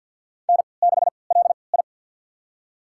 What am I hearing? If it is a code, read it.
Morse code, heard as a keyed tone, about 47 wpm, spaced as for 28 wpm, 700 Hz. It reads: N6LI
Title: USA Call Signs - 4 Characters - 3x Character Spacing - 50wpm